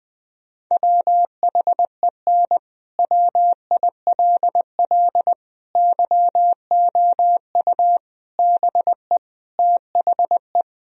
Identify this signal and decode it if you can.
Morse code, heard as a keyed tone, about 20 wpm, 700 Hz. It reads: WHEN WILL YOU BE THE
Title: Sentences from Top 1000 Words - Review Part 4 - 20wpm